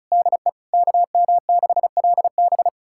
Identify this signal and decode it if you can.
Morse code, heard as a keyed tone, about 36 wpm, 700 Hz. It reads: DE KM6LB
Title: USA Callsign DE Callsign 35wpm